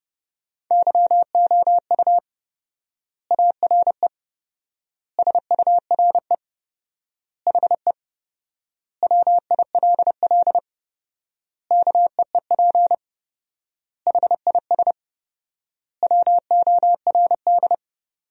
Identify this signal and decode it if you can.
Morse code, heard as a keyed tone, about 30 wpm, 700 Hz. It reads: YOU ARE SURE HE WILL KEEP HIS WORD
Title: 4x Word-Spacing - Sentences from Top 300 Words - 30wpm